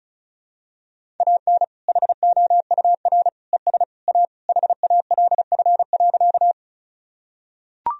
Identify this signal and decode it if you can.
Morse code, heard as a keyed tone, about 36 wpm, 700 Hz. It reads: AN HOUR ES A HALF.
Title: Sentences from Top 400 Words and Abbreviations 35wpm